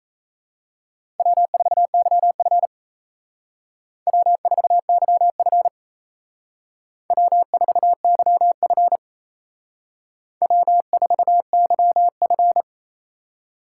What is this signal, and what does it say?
Morse code, heard as a keyed tone, about 32 wpm, 700 Hz. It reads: W4YF W4YF W4YF W4YF
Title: Speed Racing - USA 1x2 Call Signs 28wpm